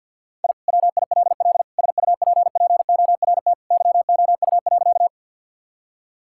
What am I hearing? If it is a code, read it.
Morse code, heard as a keyed tone, about 45 wpm, 700 Hz. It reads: I WILL SUPPORT YOU.
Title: Sentences from Top 1000 Words - Review Part 5 - 50wpm